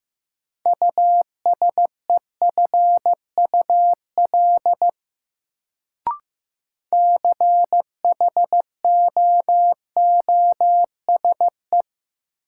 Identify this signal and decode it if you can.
Morse code, heard as a keyed tone, about 15 wpm, 700 Hz. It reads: USEFUL CHOOSE
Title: Sentences from Top 700 Words - New Word Drill - 15wpm